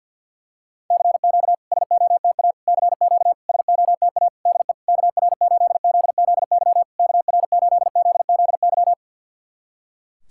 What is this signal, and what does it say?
Morse code, heard as a keyed tone, about 50 wpm, 700 Hz. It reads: CQ SOTA CQ SOTA DE KR8ZZY KR8ZZY
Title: SOTA QSO Practice 50wpm